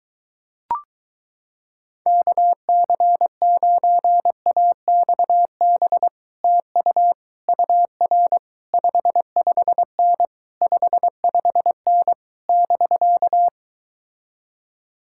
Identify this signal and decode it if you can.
Morse code, heard as a keyed tone, about 23 wpm, 700 Hz. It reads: KC9AXB TU UR 55N 55N <BK>